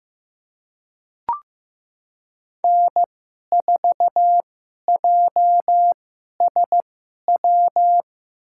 Morse code, heard as a keyed tone, about 15 wpm, spaced as for 11 wpm, 700 Hz. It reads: N4JSW
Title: USA Call Signs - 5 Characters - 2x Character Spacing - 15wpm